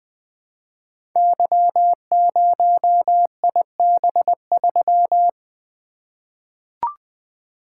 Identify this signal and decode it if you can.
Morse code, heard as a keyed tone, about 20 wpm, 700 Hz. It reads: Y0IB3
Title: Code Groups - 5 Letter:Numbers - 20wpm